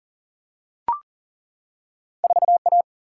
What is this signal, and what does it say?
Morse code, heard as a keyed tone, about 40 wpm, 700 Hz. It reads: 4A
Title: Single-Letter Single-Number 40wpm